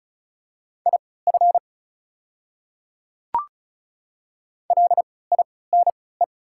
Morse code, heard as a keyed tone, about 34 wpm, spaced as for 20 wpm, 700 Hz. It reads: IF LINE
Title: 3x Character Spacing - Top 300 Words - 35wpm